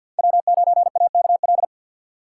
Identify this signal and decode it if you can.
Morse code, heard as a keyed tone, about 52 wpm, 700 Hz. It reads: W9AKL
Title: USA Call Signs - 5 Characters - 4x Character Spacing - 50wpm